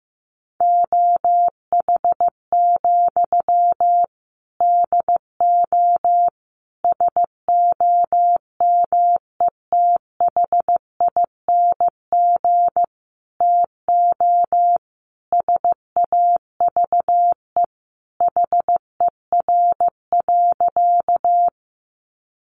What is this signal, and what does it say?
Morse code, heard as a keyed tone, about 15 wpm, 700 Hz. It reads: OH, DO SOMETHING TO SAVE HER.
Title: Sentences from Top 1100 Words 15wpm